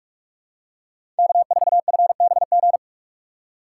Morse code, heard as a keyed tone, about 44 wpm, 700 Hz. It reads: K4FBG
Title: USA Call Signs - 5 Characters - 6x Character Spacing - 45wpm